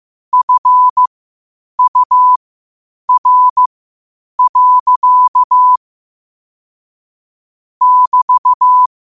Morse code, heard as a keyed tone, about 15 wpm, spaced as for 8 wpm, 1000 Hz. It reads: FUR. <BT>